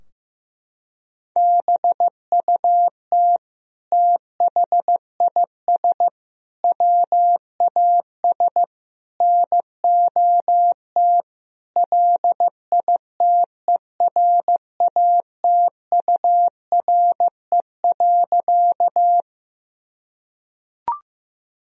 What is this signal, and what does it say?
Morse code, heard as a keyed tone, about 15 wpm, 700 Hz. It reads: BUT THIS WAS NOT LITERATURE.
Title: Sentences from Top 1000 Words - Review Part 3 - 15wpm